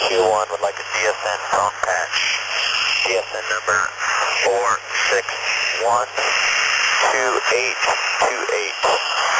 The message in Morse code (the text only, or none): none